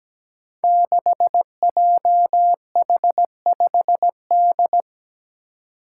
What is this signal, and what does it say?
Morse code, heard as a keyed tone, about 17 wpm, 700 Hz. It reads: 6JH5D
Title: Five Challenging Letter-Number Combinations 17wpm